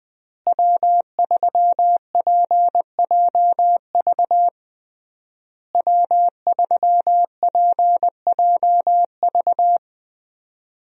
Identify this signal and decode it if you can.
Morse code, heard as a keyed tone, about 21 wpm, 700 Hz. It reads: W3PJV W3PJV
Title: USA Call Signs - 5 Characters - Repeated 3x - 20wpm